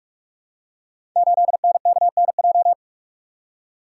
Morse code, heard as a keyed tone, about 44 wpm, 700 Hz. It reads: 8NKNJ
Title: Code Groups - 5 Letter:Numbers - 45wpm